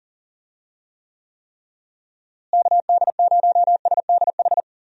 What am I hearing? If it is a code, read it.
Morse code, heard as a keyed tone, about 40 wpm, 700 Hz. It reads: KD0SDH
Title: USA Call Signs - 6 Characters - Repeated 3x - 40wpm